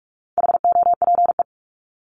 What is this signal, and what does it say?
Morse code, heard as a keyed tone, about 44 wpm, 700 Hz. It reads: HOPE
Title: Top 751-1000 Words 45wpm